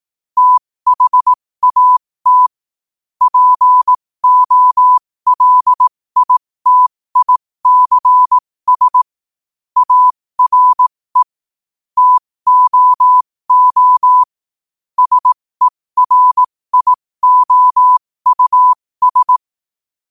Morse code, heard as a keyed tone, about 18 wpm, spaced as for 15 wpm, 1000 Hz. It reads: THAT POLITICS ARE TOO SERIOUS
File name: qotdfiles / 15wpm